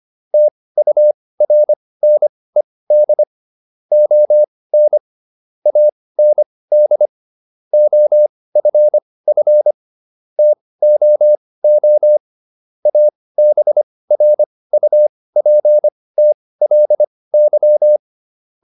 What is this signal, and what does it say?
Morse code, heard as a keyed tone, about 25 wpm, spaced as for 18 wpm, 600 Hz. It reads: TURNED ON AND OFF TOO ABRUPTLY